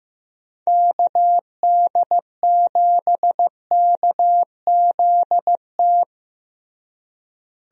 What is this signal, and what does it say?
Morse code, heard as a keyed tone, about 16 wpm, 700 Hz. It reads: KD7KZT
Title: USA Call Signs - 6 Characters - Repeated 2x - 15wpm